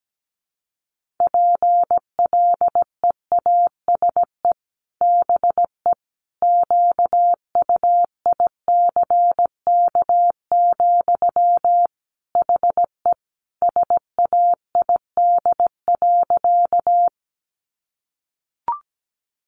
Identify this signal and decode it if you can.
Morse code, heard as a keyed tone, about 17 wpm, 700 Hz. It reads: PLEASE BE QUICK, HE SAID.
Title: Sentences from Top 1500 Words 17wpm